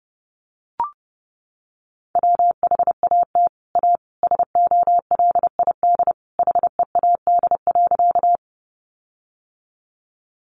Morse code, heard as a keyed tone, about 30 wpm, 700 Hz. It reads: WHAT A SOLID HEAD.
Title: Sentences from Top 1500 Words 30wpm